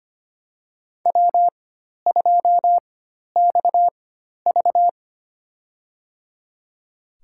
Morse code, heard as a keyed tone, about 25 wpm, spaced as for 12 wpm, 700 Hz. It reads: W2XV
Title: USA Call Signs - 4x Character Spacing - 25wpm